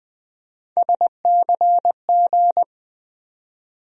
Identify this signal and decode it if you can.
Morse code, heard as a keyed tone, about 20 wpm, 700 Hz. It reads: SCG
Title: USA Call Sign Postfixes - 3 Letters 20wpm